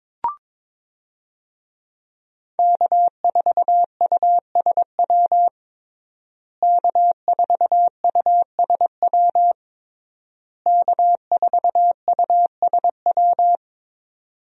Morse code, heard as a keyed tone, about 22 wpm, 700 Hz. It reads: K4USW K4USW K4USW
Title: USA Call Signs - 5 Characters - Repeated 3x - 22wpm